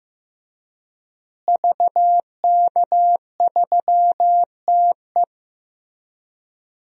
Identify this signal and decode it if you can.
Morse code, heard as a keyed tone, about 15 wpm, 700 Hz. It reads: VK3TE
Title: DXCC-100 Call Signs - 5 Characters - Repeated 3x - 15wpm